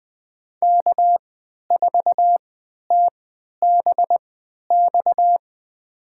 Morse code, heard as a keyed tone, about 20 wpm, spaced as for 11 wpm, 700 Hz. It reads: K4TBX
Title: USA Call Signs - 5 Characters - 3x Character Spacing - 20wpm